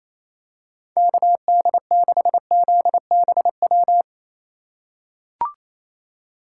Morse code, heard as a keyed tone, about 28 wpm, 700 Hz. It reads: KD6ZBW
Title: USA Call Signs - 6 Characters - 4x Character Spacing - 28wpm